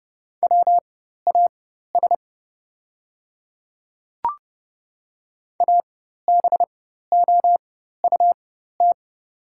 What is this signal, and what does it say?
Morse code, heard as a keyed tone, about 30 wpm, spaced as for 14 wpm, 700 Hz. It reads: WAS ABOUT
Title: 4x Character Spacing - Top 100 Words - 30wpm